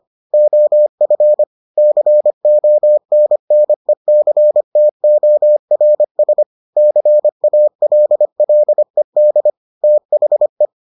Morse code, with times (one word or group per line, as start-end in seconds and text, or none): 0.3-1.4 OF
1.8-6.4 CONNECTORS
6.8-9.5 CALLED
9.8-10.7 THE